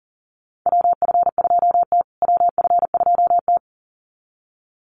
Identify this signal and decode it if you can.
Morse code, heard as a keyed tone, about 40 wpm, 700 Hz. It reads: WF2T WF2T
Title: USA Call Signs Repeated 2x 40wpm